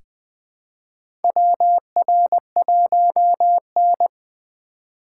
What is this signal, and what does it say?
Morse code, heard as a keyed tone, about 20 wpm, 700 Hz. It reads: WR1N
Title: USA Call Signs - 4 Characters - 3x Character Spacing - 20wpm